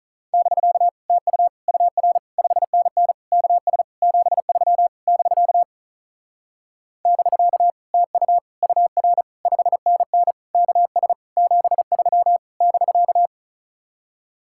Morse code, heard as a keyed tone, about 44 wpm, 700 Hz. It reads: <BK> TU UR 5NN KS 73 <BK> <BK> TU UR 5NN KS 73 <BK>